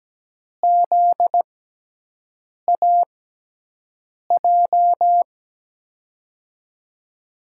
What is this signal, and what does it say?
Morse code, heard as a keyed tone, about 17 wpm, spaced as for 6 wpm, 700 Hz. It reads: ZAJ
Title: USA Call Signs - 6 Characters - 6x Character Spacing - 17wpm